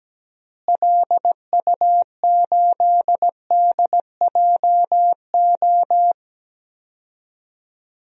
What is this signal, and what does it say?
Morse code, heard as a keyed tone, about 17 wpm, 700 Hz. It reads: LU8DJO
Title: DXCC-100 Call Signs - 6 Characters - Repeated 3x - 17wpm